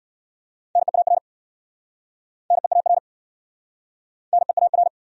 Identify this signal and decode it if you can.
Morse code, heard as a keyed tone, about 69 wpm, 700 Hz. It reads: NEAR NEAR NEAR